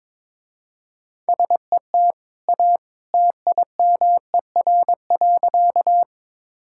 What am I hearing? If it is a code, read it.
Morse code, heard as a keyed tone, about 23 wpm, 700 Hz. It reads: SET A TIMER.
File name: practice / 22wpm / Easy Sentences with Hints - 22wpm